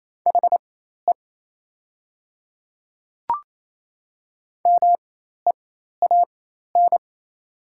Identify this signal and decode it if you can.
Morse code, heard as a keyed tone, about 27 wpm, spaced as for 13 wpm, 690 Hz. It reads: HE MEAN
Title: 4x Character Spacing - Top 200 Words - 28wpm